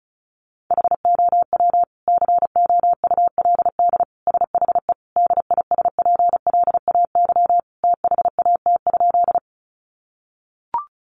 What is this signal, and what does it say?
Morse code, heard as a keyed tone, about 34 wpm, 700 Hz. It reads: HOW COULD SHE DISPLAY THAT?
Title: Sentences from Top 1200 Words 35wpm